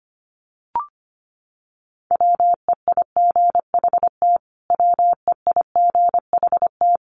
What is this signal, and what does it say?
Morse code, heard as a keyed tone, about 25 wpm, 700 Hz. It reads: WEIGHT WEIGHT
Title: Top 500 Words Repeated 25wpm